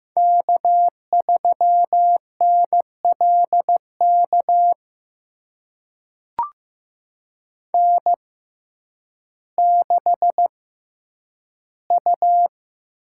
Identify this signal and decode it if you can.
Morse code, heard as a keyed tone, about 15 wpm, spaced as for 9 wpm, 700 Hz. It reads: K3NLK N6U
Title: USA Call Signs - 6x Character Spacing - 15wpm